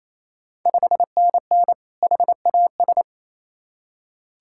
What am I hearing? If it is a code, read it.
Morse code, heard as a keyed tone, about 28 wpm, 700 Hz. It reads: 5NN HAS
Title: Ontario QSO Party Exchange 28wpm